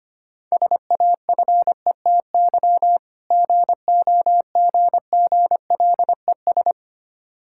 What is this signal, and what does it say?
Morse code, heard as a keyed tone, about 25 wpm, 700 Hz. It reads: SAFETY GOGGLES
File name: practice / 25wpm / Difficult Words with Hints - 25wpm